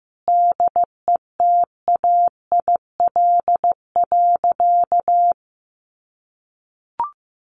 Morse code, heard as a keyed tone, about 15 wpm, 700 Hz. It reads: DETAIL.
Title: Sentences from Top 1500 Words 15wpm